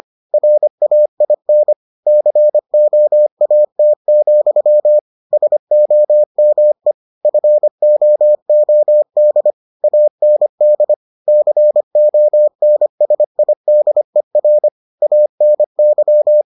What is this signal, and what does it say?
Morse code, heard as a keyed tone, about 25 wpm, 600 Hz. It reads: RAIN COAT, SOME FOOD AND CONSIDER ANY